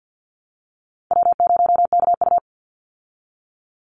Morse code, heard as a keyed tone, about 50 wpm, 700 Hz. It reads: W0KU
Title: Easy Call Signs Repeated 3x - 50wpm